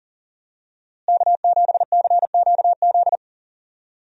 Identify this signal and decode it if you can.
Morse code, heard as a keyed tone, about 40 wpm, 700 Hz. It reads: K7CQZ
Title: USA Call Signs - 5 Characters - Repeated 3x - 40wpm